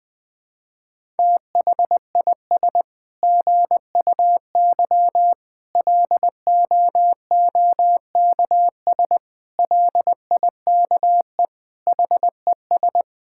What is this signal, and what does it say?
Morse code, heard as a keyed tone, about 21 wpm, 700 Hz. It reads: THIS GUY LOOKS LIKE HES
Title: Sets of 9 Words - 20wpm